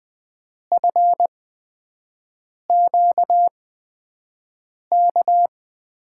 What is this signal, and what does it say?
Morse code, heard as a keyed tone, about 20 wpm, spaced as for 5 wpm, 700 Hz. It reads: FQK
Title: USA Call Signs - 5 Characters - 8x Character Spacing - 20wpm